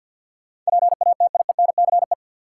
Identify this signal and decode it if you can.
Morse code, heard as a keyed tone, about 52 wpm, 700 Hz. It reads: PATIENCE
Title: Sentences from Top 1300 Words - New Word Drill - 50wpm